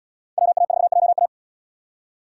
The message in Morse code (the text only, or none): WA4JA